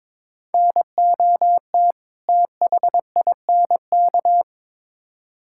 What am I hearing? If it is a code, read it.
Morse code, heard as a keyed tone, about 22 wpm, 700 Hz. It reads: NOT THINK
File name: Build A Phrase - 2-Word Phrases - 22wpm